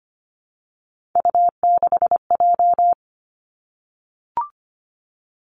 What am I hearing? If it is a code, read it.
Morse code, heard as a keyed tone, about 25 wpm, 700 Hz. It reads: U6J
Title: Three Challenging Letter-Number Combinations 25wpm